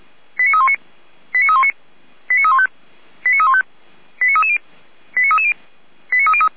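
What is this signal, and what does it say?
Five-tone selective call (ZVEI), heard as quick runs of five tones, tones of about 80 ms. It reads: 89319 89319 89315 89315 89330 89330
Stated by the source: ZVEI five-tone selective call received by SDR.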